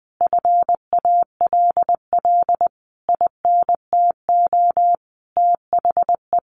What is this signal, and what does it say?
Morse code, heard as a keyed tone, about 20 wpm, 700 Hz. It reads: FALL INTO THE
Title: Sentences from Top 1300 Words 20wpm